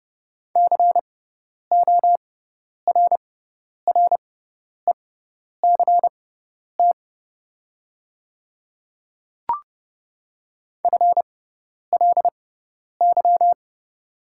Morse code, heard as a keyed tone, about 30 wpm, spaced as for 10 wpm, 700 Hz. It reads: CORRECT FLY
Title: 6x Character Spacing - Top 500 Words - 30wpm